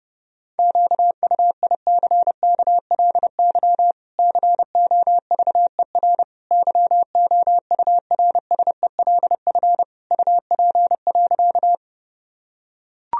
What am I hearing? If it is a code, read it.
Morse code, heard as a keyed tone, about 30 wpm, 700 Hz. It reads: QUICKLY COVER YOURSELF UP.
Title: Sentences from Top 1000 Words - Review Part 1 - 30wpm